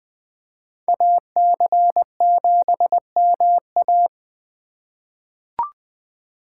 Morse code, heard as a keyed tone, about 20 wpm, 700 Hz. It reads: AC7MA
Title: 2023 SOTA Chaser Call Signs - 20wpm